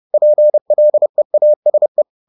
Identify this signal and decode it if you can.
Morse code, heard as a keyed tone, about 30 wpm, 600 Hz. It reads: PLEASE
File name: the old man and the sea / the old man and the sea 1 30wpm